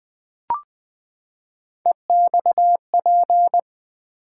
Morse code, heard as a keyed tone, about 20 wpm, 700 Hz. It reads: EXP
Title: Build A Word - 3 Letters At A Time - 9-Letter Words - 20wpm